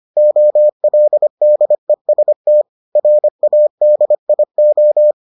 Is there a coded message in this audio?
Yes: Morse code from 0.2 to 5.1 s, heard as a keyed tone, about 25 wpm, 600 Hz. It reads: OLDEST RADIO